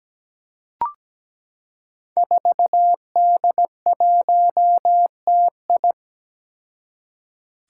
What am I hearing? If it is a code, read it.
Morse code, heard as a keyed tone, about 17 wpm, 700 Hz. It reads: 4D1TI
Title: Five Challenging Letter-Number Combinations 17wpm